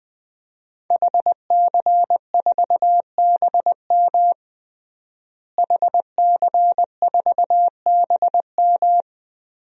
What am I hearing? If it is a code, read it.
Morse code, heard as a keyed tone, about 21 wpm, 700 Hz. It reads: HC4BM HC4BM
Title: DXCC-100 Call Signs - 5 Characters - Repeated 3x - 20wpm